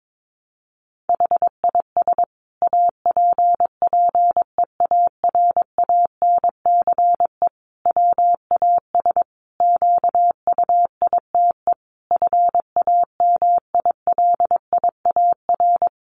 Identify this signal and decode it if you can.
Morse code, heard as a keyed tone, about 22 wpm, 700 Hz. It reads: HIS APPEARANCE WAS QUITE FAMILIAR
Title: Sentences from Top 1100 Words 22wpm